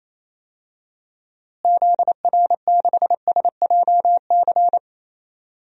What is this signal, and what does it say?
Morse code, heard as a keyed tone, about 28 wpm, 700 Hz. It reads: ZR6SJC
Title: DXCC-100 Call Signs - 6 Characters - Repeated 3x - 28wpm